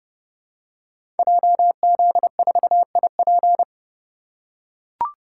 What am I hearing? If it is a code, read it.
Morse code, heard as a keyed tone, about 30 wpm, 700 Hz. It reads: JZ4IP
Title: Code Groups - 5 Letter:Numbers - 30wpm